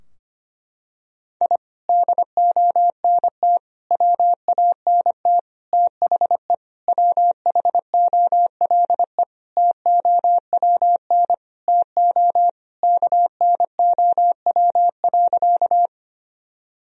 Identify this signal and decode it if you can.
Morse code, heard as a keyed tone, about 25 wpm, 700 Hz. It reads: I DONT WANT THE WHOLE TOWN TO KNOW.
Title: Sentences from Top 1000 Words - Review Part 2 - 25wpm